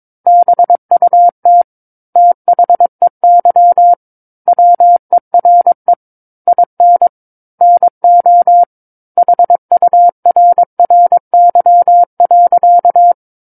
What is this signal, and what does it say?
Morse code, heard as a keyed tone, about 22 wpm, 700 Hz. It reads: BUT THEY WERE IN NO HURRY.
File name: Book One / 13-Chapter 13 - 22 WPM